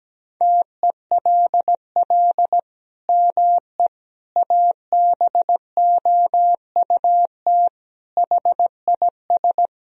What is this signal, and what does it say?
Morse code, heard as a keyed tone, about 17 wpm, 700 Hz. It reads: TELL ME ABOUT HIS
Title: Sentences from Top 1000 Words - Review Part 3 - 17wpm